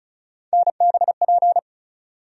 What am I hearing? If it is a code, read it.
Morse code, heard as a keyed tone, about 36 wpm, 700 Hz. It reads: NBP
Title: USA Call Sign Postfixes - 3 Letters 35wpm